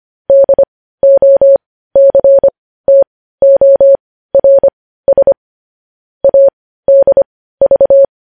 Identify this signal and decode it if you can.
Morse code, heard as a keyed tone, about 25 wpm, spaced as for 15 wpm, 550 Hz. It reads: DOCTORS ADV
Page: News headlines in morse code at 15 wpm